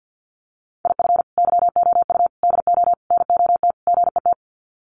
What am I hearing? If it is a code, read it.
Morse code, heard as a keyed tone, about 50 wpm, 700 Hz. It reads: IF YOU DO NOT GET
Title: Sets of 5 Words Encore2 50wpm